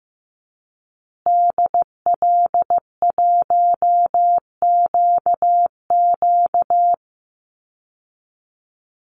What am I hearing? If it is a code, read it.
Morse code, heard as a keyed tone, about 15 wpm, 700 Hz. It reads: DL1QQ
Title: Call Signs Repeated 2x - 15wpm